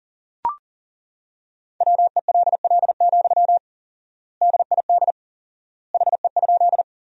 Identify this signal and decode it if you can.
Morse code, heard as a keyed tone, about 40 wpm, 700 Hz. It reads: WELL, DID HE?
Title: 4x Word-Spacing - Sentences from Top 200 Words - 40wpm